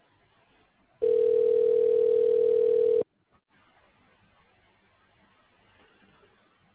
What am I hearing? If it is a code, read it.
Ringback tone (North America), 440 + 480 Hz.